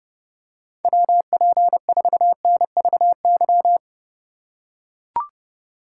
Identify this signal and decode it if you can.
Morse code, heard as a keyed tone, about 28 wpm, 700 Hz. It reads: WP4NVY